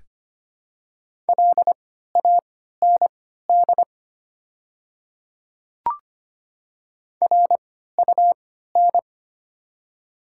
Morse code, heard as a keyed tone, about 25 wpm, spaced as for 14 wpm, 700 Hz. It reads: LAND RUN
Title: 3x Character Spacing - Top 500 Words - 25wpm